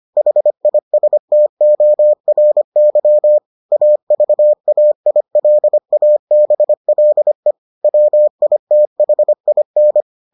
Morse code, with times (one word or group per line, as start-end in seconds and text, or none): 0.2-3.4 HISTORY
3.7-7.5 AVAILABLE
7.8-10.0 WITHIN